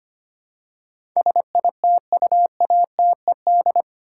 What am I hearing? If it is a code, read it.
Morse code, heard as a keyed tone, about 25 wpm, 700 Hz. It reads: SITUATED
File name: 8 Letter Words - First 3 Letters Given - 25wpm